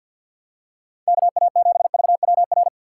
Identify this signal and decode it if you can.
Morse code, heard as a keyed tone, about 52 wpm, 700 Hz. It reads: KA7VWR
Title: USA Call Signs - 6x Character Spacing - 50wpm